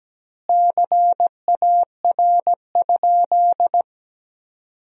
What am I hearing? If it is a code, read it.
Morse code, heard as a keyed tone, about 17 wpm, 700 Hz. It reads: CAR?
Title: 2x Word-Spacing - Sentences from Top 300 Words - 17wpm